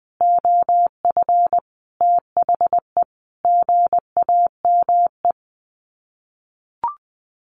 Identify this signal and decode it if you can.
Morse code, heard as a keyed tone, about 20 wpm, 700 Hz. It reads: OF THE GAME
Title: Sets of 5 Words Encore2 20wpm